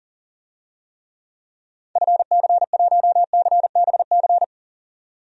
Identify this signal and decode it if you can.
Morse code, heard as a keyed tone, about 40 wpm, 700 Hz. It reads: FC1CBC
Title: DXCC-100 Call Signs - 6 Characters - 40wpm